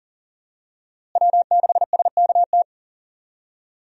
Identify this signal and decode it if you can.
Morse code, heard as a keyed tone, about 40 wpm, 700 Hz. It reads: W6SKT
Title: USA Call Signs - 5 Characters - Repeated 2x - 40wpm